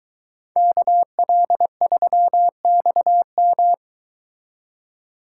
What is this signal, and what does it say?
Morse code, heard as a keyed tone, about 22 wpm, 700 Hz. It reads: KL3XM